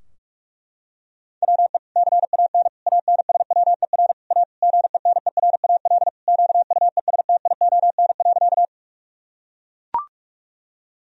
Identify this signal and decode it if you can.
Morse code, heard as a keyed tone, about 41 wpm, 700 Hz. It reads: WE CAN ANSWER A GENERAL QUESTION.
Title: Sentences from Top 1000 Words - Review Part 3 - 45wpm